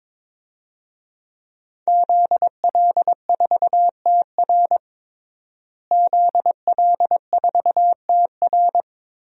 Morse code, heard as a keyed tone, about 21 wpm, 700 Hz. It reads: ZL4TR ZL4TR
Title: DXCC-100 Call Signs - 5 Characters - Repeated 2x - 22wpm